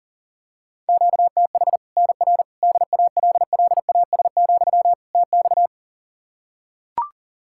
Morse code, heard as a keyed tone, about 40 wpm, 700 Hz. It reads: QTH NR DALLAS, TX
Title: Practice QSO - Exchange 1 - Beginning - 40wpm